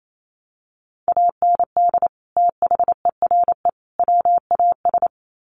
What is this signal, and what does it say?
Morse code, heard as a keyed tone, about 28 wpm, 700 Hz. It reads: AND THERE WAS
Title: Sets of 3 Words 28wpm